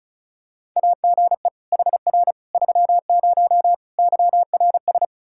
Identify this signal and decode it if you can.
Morse code, heard as a keyed tone, about 34 wpm, 700 Hz. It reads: AGE HR 30 YRS
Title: Practice QSO - Exchange 3 - More Info - 35wpm